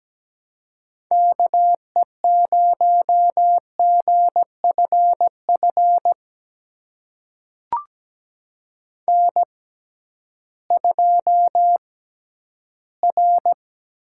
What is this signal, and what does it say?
Morse code, heard as a keyed tone, about 17 wpm, spaced as for 11 wpm, 700 Hz. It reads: KE0GFF N2R
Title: USA Call Signs - 6x Character Spacing - 17wpm